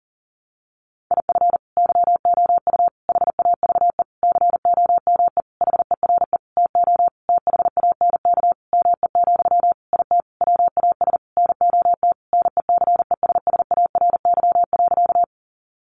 Morse code, heard as a keyed tone, about 40 wpm, 700 Hz. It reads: IF YOU HAVE COME HERE TO THANK ME, IT WAS NOT NECESSARY.
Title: Sentences from Top 800 Words 40wpm